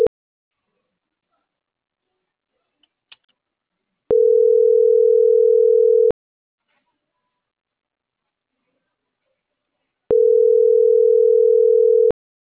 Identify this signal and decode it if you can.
Ringback tone (North America), 440 + 480 Hz.